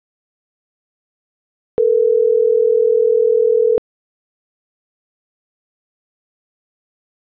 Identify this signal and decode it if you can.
Ringback tone (North America), 440 + 480 Hz.